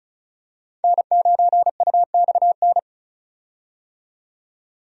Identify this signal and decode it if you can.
Morse code, heard as a keyed tone, about 36 wpm, 700 Hz. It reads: N9UXN
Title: USA Call Signs - 5 Characters - 35wpm